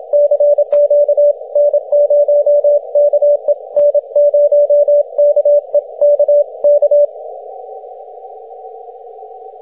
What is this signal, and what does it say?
Morse code, heard as a keyed tone, about 27 wpm, 580 Hz. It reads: CQ N0KE N0KE K K
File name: wp / n0ke 6m cw june 7th 2010